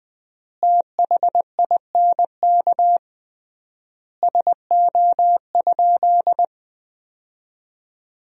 Morse code, heard as a keyed tone, about 20 wpm, 700 Hz. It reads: THINK SO?
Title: 3x Word-Spacing - Sentences from Top 200 Words - 20wpm